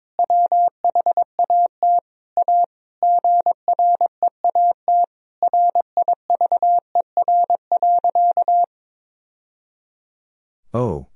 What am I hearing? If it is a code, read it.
Morse code, heard as a keyed tone, about 22 wpm, 700 Hz. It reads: WHAT A GREAT RIVER.
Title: Sentences from Top 1000 Words - Review Part 1 - 22wpm